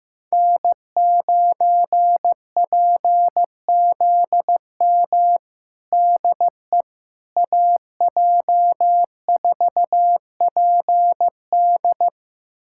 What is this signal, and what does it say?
Morse code, heard as a keyed tone, about 16 wpm, 700 Hz. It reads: N9PZM DE AJ4PD
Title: USA Callsign DE Callsign - 5 Characters - 15wpm